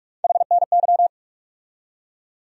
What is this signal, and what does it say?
Morse code, heard as a keyed tone, about 46 wpm, 700 Hz. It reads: HNY
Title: USA Call Sign Postfixes - 3 Letters 45wpm